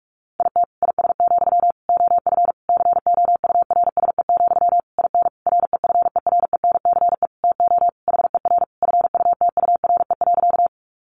Morse code, heard as a keyed tone, about 44 wpm, 700 Hz. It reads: IT IS, OF COURSE, IN REFERENCE TO HER FUTURE.
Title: Sentences from Top 1200 Words 45wpm